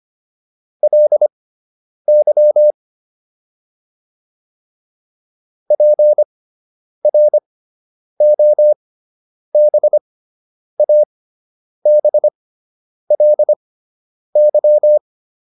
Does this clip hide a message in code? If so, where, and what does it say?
Yes: Morse code from 0.8 to 15.0 s, heard as a keyed tone, about 25 wpm, spaced as for 9 wpm, 600 Hz. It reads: LY PROBABLY